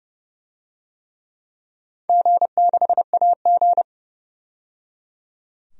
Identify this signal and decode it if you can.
Morse code, heard as a keyed tone, about 30 wpm, 700 Hz. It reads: G6AG
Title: DXCC-100 Call Signs - 4 Characters - 30wpm